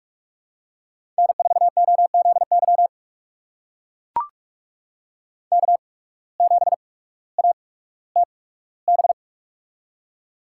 Morse code, heard as a keyed tone, about 44 wpm, spaced as for 19 wpm, 700 Hz. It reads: N4OZY K7ATB
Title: USA Call Signs - 5 Characters - 8x Character Spacing - 45wpm